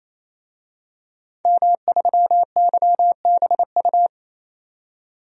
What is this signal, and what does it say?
Morse code, heard as a keyed tone, about 28 wpm, 700 Hz. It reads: M3YBU
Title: DXCC-100 Call Signs - 5 Characters - 28wpm